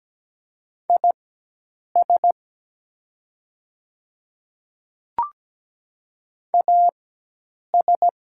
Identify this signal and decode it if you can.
Morse code, heard as a keyed tone, about 17 wpm, spaced as for 8 wpm, 700 Hz. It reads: IS AS
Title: 4x Character Spacing - Top 200 Words - 17wpm